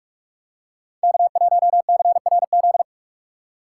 Morse code, heard as a keyed tone, about 46 wpm, 690 Hz. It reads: K1CRZ
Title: USA Call Signs - 5 Characters - 6x Character Spacing - 45wpm